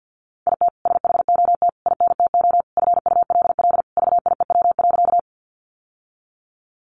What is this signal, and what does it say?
Morse code, heard as a keyed tone, about 50 wpm, 700 Hz. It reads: IT SHOT INTO FULL VIEW.
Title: Sentences from Top 1100 Words 50wpm